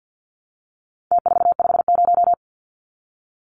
Morse code, heard as a keyed tone, about 50 wpm, 700 Hz. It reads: T450
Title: Numbers - Contest Style - 4-Digit 50wpm